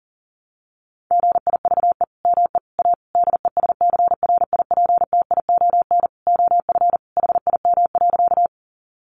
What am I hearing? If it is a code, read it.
Morse code, heard as a keyed tone, about 40 wpm, 700 Hz. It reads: GIVE ME A DESCRIPTION OF HIM.
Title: Sentences from Top 1300 Words 40wpm